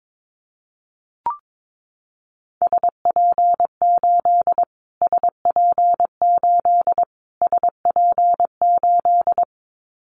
Morse code, heard as a keyed tone, about 22 wpm, 700 Hz. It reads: SP8 SP8 SP8
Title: Callsign Prefixes for Top 25 DXCC - Repeated 3x - 22wpm